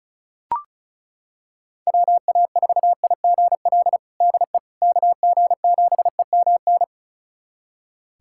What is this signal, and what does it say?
Morse code, heard as a keyed tone, about 36 wpm, 700 Hz. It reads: WA4IGL DE KG7EMN
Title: USA Callsign DE Callsign - 6 Characters - 35wpm